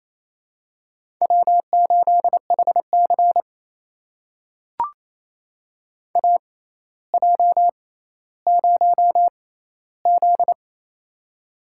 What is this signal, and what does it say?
Morse code, heard as a keyed tone, about 28 wpm, spaced as for 14 wpm, 700 Hz. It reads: W8HC AJ0Z
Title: USA Call Signs - 4 Characters - 6x Character Spacing - 28wpm